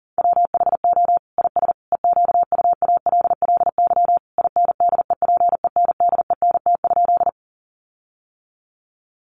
Morse code, heard as a keyed tone, about 40 wpm, 700 Hz. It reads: WHO IS EQUALLY INDEPENDENT?
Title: Sentences from Top 1400 Words 40wpm